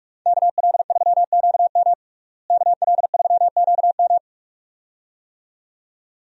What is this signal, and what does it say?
Morse code, heard as a keyed tone, about 46 wpm, 700 Hz. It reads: KL3QM KL3QM
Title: USA Call Signs - 5 Characters - Repeated 2x - 45wpm